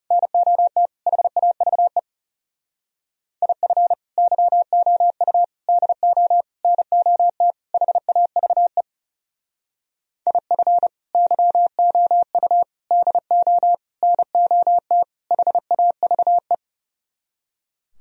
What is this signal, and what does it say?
Morse code, heard as a keyed tone, about 37 wpm, 700 Hz. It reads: NOT HAVE IF YOU DO NOT HAVE IF YOU DO NOT HAVE
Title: Speed-Racing - Sets of 5 Words 30wpm